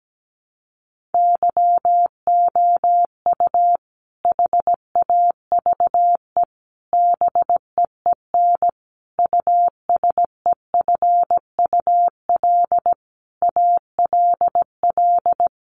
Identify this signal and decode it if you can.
Morse code, heard as a keyed tone, about 17 wpm, 700 Hz. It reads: YOU HAVE BEEN USEFUL ALL
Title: Sentences from Top 700 Words 17wpm